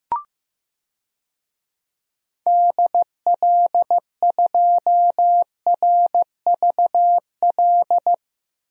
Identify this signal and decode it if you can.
Morse code, heard as a keyed tone, about 15 wpm, 700 Hz. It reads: DL2RVL
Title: DXCC-100 Call Signs - 6 Characters - 15wpm